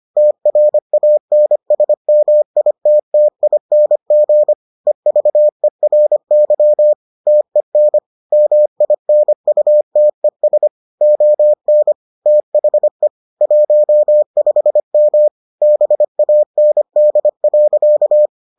Morse code, heard as a keyed tone, about 25 wpm, 600 Hz. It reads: TRANSMITTING EVERY TEN MINUTES ON THE 15M BAND.